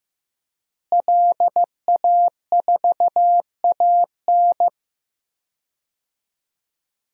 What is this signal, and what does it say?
Morse code, heard as a keyed tone, about 15 wpm, 700 Hz. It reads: LA4AN
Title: DXCC-100 Call Signs - 5 Characters - Repeated 2x - 15wpm